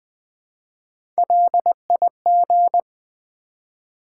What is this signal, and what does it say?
Morse code, heard as a keyed tone, about 20 wpm, 700 Hz. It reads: LIG
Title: Build A Word - 1 Letter At A Time - 3 to 5-Letter Words - 20wpm